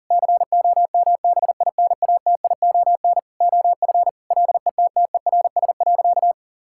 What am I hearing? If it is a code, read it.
Morse code, heard as a keyed tone, about 40 wpm, 700 Hz. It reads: COMBINATION OF LETTERS.
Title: Sentences from Top 1000 Words - Review Part 6 - 40wpm